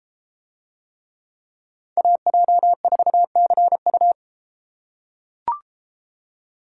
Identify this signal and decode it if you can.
Morse code, heard as a keyed tone, about 32 wpm, 700 Hz. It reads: AJ4CU